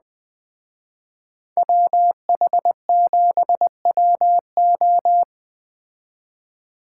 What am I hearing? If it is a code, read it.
Morse code, heard as a keyed tone, about 20 wpm, 700 Hz. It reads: WH7WO
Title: Speed Racing - USA 2x2 Call Signs 17wpm